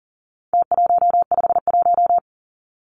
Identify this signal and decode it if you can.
Morse code, heard as a keyed tone, about 40 wpm, 700 Hz. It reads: T151
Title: Numbers - Contest Style - 4-Digit 40wpm